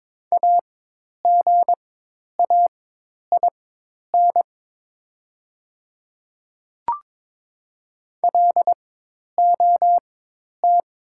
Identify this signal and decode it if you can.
Morse code, heard as a keyed tone, about 22 wpm, spaced as for 10 wpm, 700 Hz. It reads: AGAIN LOT
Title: 4x Character Spacing - Top 500 Words - 22wpm